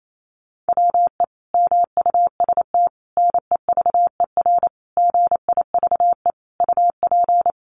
Morse code, heard as a keyed tone, about 28 wpm, 700 Hz. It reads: WE MUST NEVER GIVE UP
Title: Sentences from Top 1500 Words 28wpm